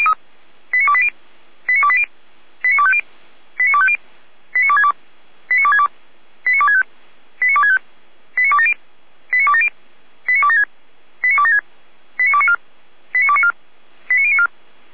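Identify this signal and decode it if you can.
Five-tone selective call (ZVEI), heard as quick runs of five tones, tones of about 70 ms. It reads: 89280 89280 89250 89250 89262 89262 89265 89265 89270 89270 89276 89276 89294 89294 89094
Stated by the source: Received by SDR; ZVEI five-tone call.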